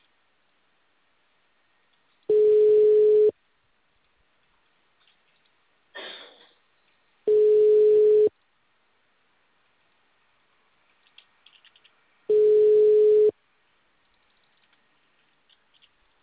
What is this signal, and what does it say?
Ringback tone (Europe, 425 Hz).
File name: out- -2004- - - .7141